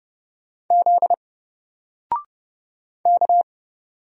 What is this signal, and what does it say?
Morse code, heard as a keyed tone, about 30 wpm, 700 Hz. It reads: Z K
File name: Single-Letter-Number-30wpm